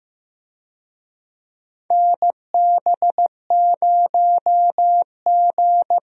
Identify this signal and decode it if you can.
Morse code, heard as a keyed tone, about 16 wpm, 700 Hz. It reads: NB0G
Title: USA Call Signs - 4 Characters - Repeated 3x - 15wpm